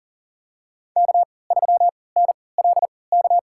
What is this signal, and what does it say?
Morse code, heard as a keyed tone, about 40 wpm, spaced as for 23 wpm, 700 Hz. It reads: K3NLK
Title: USA Call Signs - 5 Characters - 3x Character Spacing - 40wpm